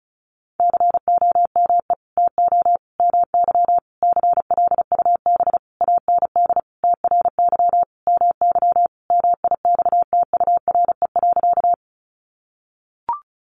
Morse code, heard as a keyed tone, about 34 wpm, 700 Hz. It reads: COME TO MY CLUB AND TRY MY MIXTURE.
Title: Sentences from Top 1500 Words 35wpm